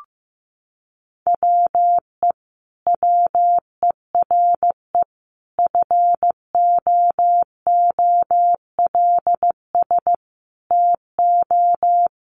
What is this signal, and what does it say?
Morse code, heard as a keyed tone, about 15 wpm, 700 Hz. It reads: WE WERE FOOLS TO
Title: A Sound Of Thunder 15wpm